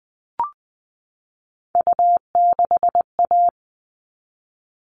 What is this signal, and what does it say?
Morse code, heard as a keyed tone, about 20 wpm, 700 Hz. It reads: U6A
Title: All Cut Numbers - 3-Digit 20wpm